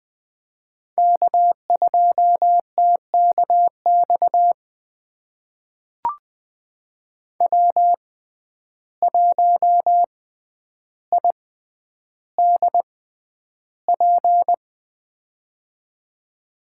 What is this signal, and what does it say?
Morse code, heard as a keyed tone, about 20 wpm, spaced as for 10 wpm, 700 Hz. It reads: K2TKX W1IDP
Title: USA Call Signs - 5 Characters - 6x Character Spacing - 20wpm